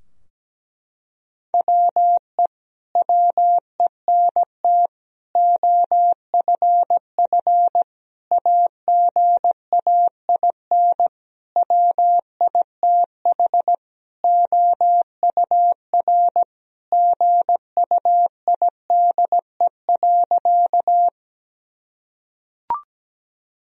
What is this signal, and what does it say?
Morse code, heard as a keyed tone, about 17 wpm, 700 Hz. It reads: WE WENT OFF AGAIN WITH OUR GUIDE.
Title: Sentences from Top 1000 Words - Review Part 3 - 17wpm